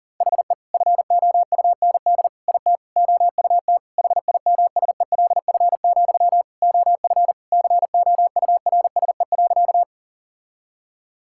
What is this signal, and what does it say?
Morse code, heard as a keyed tone, about 40 wpm, 700 Hz. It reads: HE FOUND IT OUT HIMSELF, OF COURSE.
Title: Sentences from Top 1600 Words 40wpm